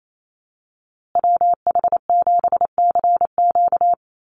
Morse code, heard as a keyed tone, about 28 wpm, 700 Hz. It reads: WH7CQ
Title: USA Call Signs Repeated 2x 28wpm